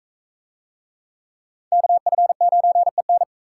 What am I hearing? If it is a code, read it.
Morse code, heard as a keyed tone, about 41 wpm, 700 Hz. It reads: KF9EN